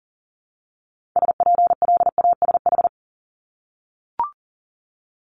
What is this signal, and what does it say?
Morse code, heard as a keyed tone, about 41 wpm, 690 Hz. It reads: SPLASH
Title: Multi-letter Phonemes 40wpm